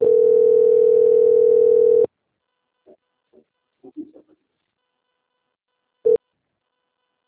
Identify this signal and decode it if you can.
Ringback tone (North America), 440 + 480 Hz.